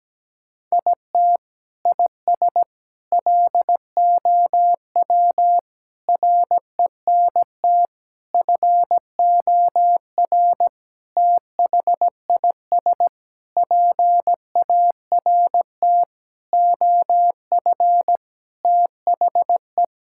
Morse code, heard as a keyed tone, about 17 wpm, 700 Hz. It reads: IT IS LOW RENT FOR THIS PART OF THE
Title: Sentences from Top 1000 Words - Review Part 5 - 17wpm